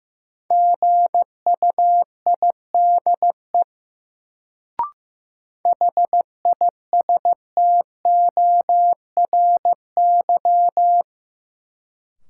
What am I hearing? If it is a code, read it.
Morse code, heard as a keyed tone, about 15 wpm, 700 Hz. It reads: GUIDE HISTORY
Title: Sentences from Top 600 Words - New Word Drill - 15wpm